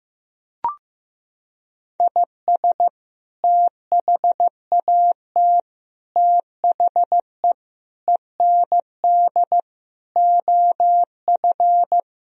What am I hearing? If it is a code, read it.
Morse code, heard as a keyed tone, about 15 wpm, 700 Hz. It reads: IS THAT THE END OF
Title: Sentences from Top 1000 Words - Review Part 1 - 15wpm